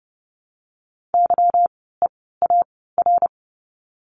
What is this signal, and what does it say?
Morse code, heard as a keyed tone, about 30 wpm, spaced as for 17 wpm, 700 Hz. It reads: YEAR
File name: course / 30:10wpm / 230 Lesson 37f - Letter x - Words 30wpm